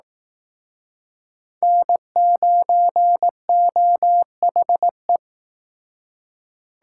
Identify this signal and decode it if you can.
Morse code, heard as a keyed tone, about 18 wpm, 700 Hz. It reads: N9OHE
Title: Speed Racing - USA 1x3 Call Signs 15wpm